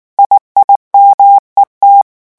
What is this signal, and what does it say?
Morse code, heard as a keyed tone, about 19 wpm, 800 Hz. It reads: IIMET
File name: morse mp3 / revision7-vitesse 18 mots